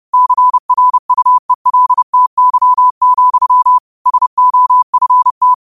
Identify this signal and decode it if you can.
Morse code, heard as a keyed tone, about 30 wpm, 1000 Hz. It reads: CRUELTY, SOFT
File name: qotdfiles / 30wpm